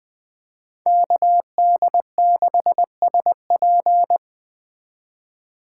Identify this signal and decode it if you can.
Morse code, heard as a keyed tone, about 20 wpm, 700 Hz. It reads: KD6SP
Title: Speed Racing - USA 2x2 Call Signs 17wpm